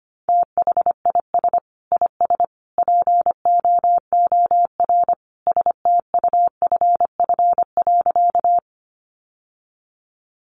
Morse code, heard as a keyed tone, about 25 wpm, 700 Hz. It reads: THIS IS POOR STUFF.
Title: Sentences from Top 1400 Words 25wpm